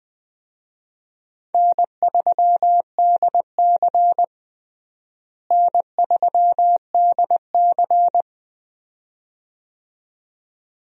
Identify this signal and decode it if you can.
Morse code, heard as a keyed tone, about 20 wpm, 700 Hz. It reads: N3DC N3DC
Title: USA Call Signs - 4 Characters - Repeated 2x - 20wpm